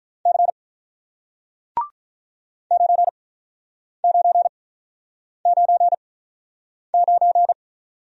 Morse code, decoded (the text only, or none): C 9999